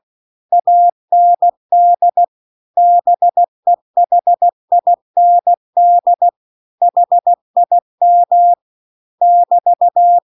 Morse code, heard as a keyed tone, about 16 wpm, 700 Hz. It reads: AND BEHIND HIM <BT>
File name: CW-The-Big-Twoheaded-River-2-16WPM